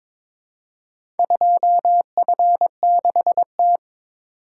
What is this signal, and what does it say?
Morse code, heard as a keyed tone, about 23 wpm, 700 Hz. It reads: 2F6T
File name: practice / 22wpm / Four Letter-Number Combinations - 22wpm